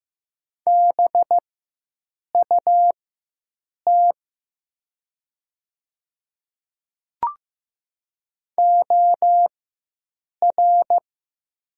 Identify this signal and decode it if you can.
Morse code, heard as a keyed tone, about 15 wpm, spaced as for 7 wpm, 700 Hz. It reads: BUT OR
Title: 4x Character Spacing - Top 100 Words - 15wpm